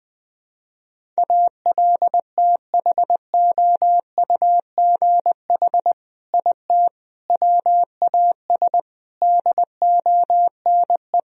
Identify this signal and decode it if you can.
Morse code, heard as a keyed tone, about 20 wpm, 700 Hz. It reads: ALTHOUGH IT WAS DONE
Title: Sentences from Top 1000 Words - Review Part 3 - 20wpm